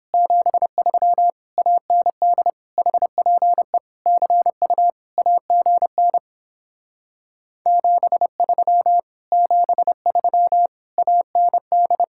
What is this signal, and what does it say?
Morse code, heard as a keyed tone, about 29 wpm, 700 Hz. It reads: 73 AND HPE CU AGN 73 73 AND